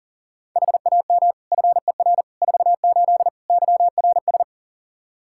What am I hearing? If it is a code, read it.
Morse code, heard as a keyed tone, about 40 wpm, 700 Hz. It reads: HAM FER 48 YRS
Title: K1USN SST QSO Practice 40wpm